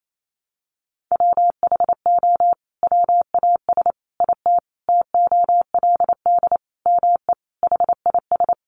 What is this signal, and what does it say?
Morse code, heard as a keyed tone, about 28 wpm, 700 Hz. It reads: WHO WAS IT TOLD ME HIS
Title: Sentences from Top 400 Words 28wpm